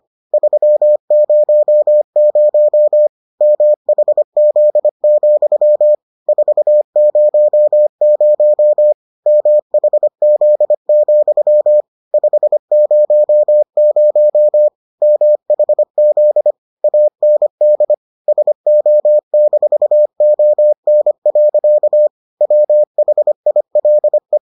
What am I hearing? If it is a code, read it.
Morse code, heard as a keyed tone, about 25 wpm, 600 Hz. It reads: 300 MHZ, 400 MHZ, 500 MHZ AND SO-ON. WHILE